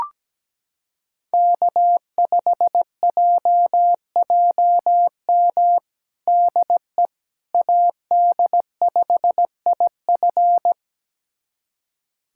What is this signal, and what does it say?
Morse code, heard as a keyed tone, about 17 wpm, 700 Hz. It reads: K5JJM DE AD5IF
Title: USA Callsign DE Callsign - 5 Characters - 17wpm